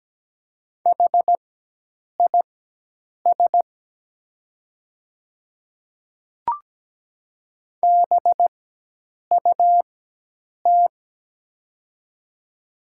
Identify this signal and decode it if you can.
Morse code, heard as a keyed tone, about 17 wpm, spaced as for 8 wpm, 700 Hz. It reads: HIS BUT
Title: 4x Character Spacing - Top 300 Words - 17wpm